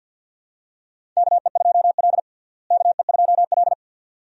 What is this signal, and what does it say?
Morse code, heard as a keyed tone, about 52 wpm, 700 Hz. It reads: KE2L KE2L
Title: USA Call Signs - 4 Characters - Repeated 3x - 50wpm